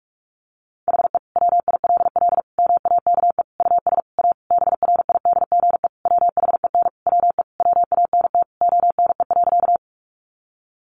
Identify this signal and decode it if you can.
Morse code, heard as a keyed tone, about 44 wpm, 700 Hz. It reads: HE WILL MAKE US A BRIDGE WHEN WE WANT ONE.
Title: Sentences from Top 1200 Words 45wpm